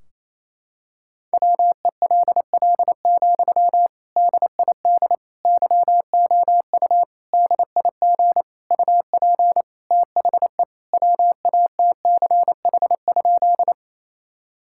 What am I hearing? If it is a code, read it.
Morse code, heard as a keyed tone, about 27 wpm, 700 Hz. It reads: WELL, DID YOU DIG UP THE WATCH?
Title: Sentences from Top 1600 Words 28wpm